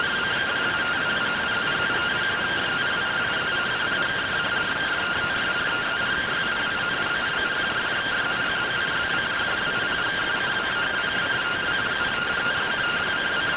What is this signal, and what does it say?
Two-tone teleprinter signal, about 170 Hz shift.